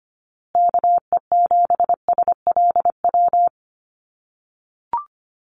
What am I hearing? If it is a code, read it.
Morse code, heard as a keyed tone, about 25 wpm, 700 Hz. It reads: KE7SLW
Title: USA Call Signs Repeated 2x 25wpm